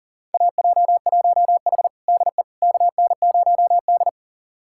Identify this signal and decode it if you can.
Morse code, heard as a keyed tone, about 40 wpm, 700 Hz. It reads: AJ1H DE KN0D
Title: USA Callsign DE Callsign - 4 Characters - 40wpm